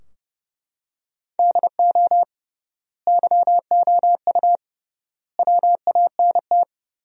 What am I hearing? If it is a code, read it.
Morse code, heard as a keyed tone, about 30 wpm, 700 Hz. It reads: DO YOU WANT
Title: 3x Word-Spacing - Sentences from Top 300 Words - 30wpm